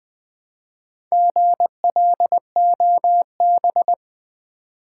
Morse code, heard as a keyed tone, about 20 wpm, 700 Hz. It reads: GLOB
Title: Build A Word - 1 Letter At A Time - 5-Letter Words - 20wpm